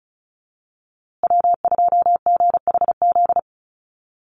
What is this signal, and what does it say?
Morse code, heard as a keyed tone, about 34 wpm, 700 Hz. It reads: W2GHZ
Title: USA Call Signs Repeated 2x 35wpm